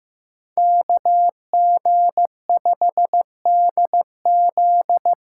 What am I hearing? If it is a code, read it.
Morse code, heard as a keyed tone, about 15 wpm, 700 Hz. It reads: KG5DZ
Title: USA Call Signs - 6 Characters - 8x Character Spacing - 15wpm